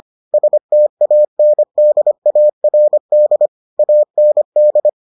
Morse code, heard as a keyed tone, about 25 wpm, 600 Hz. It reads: STANDARD AND